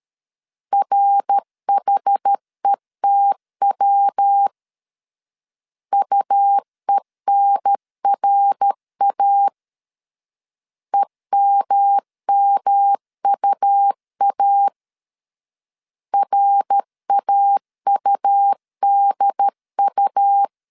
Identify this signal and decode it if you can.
Morse code, heard as a keyed tone, about 13 wpm, 800 Hz. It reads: RHETW UENRA EMMUA RAUDU
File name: CW / DET / 1 3 1500C S C 12wpm 2750